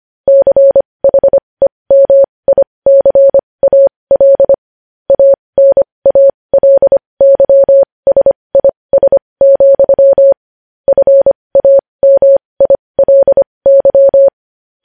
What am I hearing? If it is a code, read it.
Morse code, heard as a keyed tone, about 25 wpm, spaced as for 20 wpm, 550 Hz. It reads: CHEMICAL ANALYSIS, FAMILY
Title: News Headlines at 20